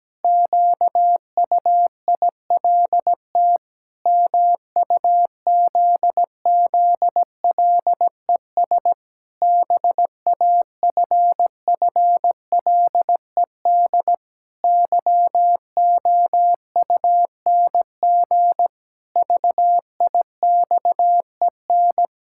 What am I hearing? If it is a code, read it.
Morse code, heard as a keyed tone, about 17 wpm, 700 Hz. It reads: QUILT MUZZLES BAFFLED YOUNG VIXEN
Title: Pangram Challenge v2 - 17wpm